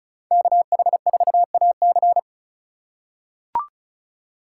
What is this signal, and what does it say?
Morse code, heard as a keyed tone, about 36 wpm, 700 Hz. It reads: KH4AC
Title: USA Call Signs - 5 Characters - 3x Character Spacing - 35wpm